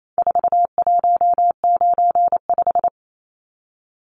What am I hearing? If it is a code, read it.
Morse code, heard as a keyed tone, about 28 wpm, 700 Hz. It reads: 4195
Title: Numbers - 4-Digit 28wpm